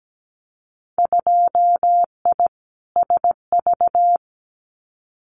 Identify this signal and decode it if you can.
Morse code, heard as a keyed tone, about 17 wpm, 700 Hz. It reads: 2I SV
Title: Winter Field Day Exchanges Encore 17wpm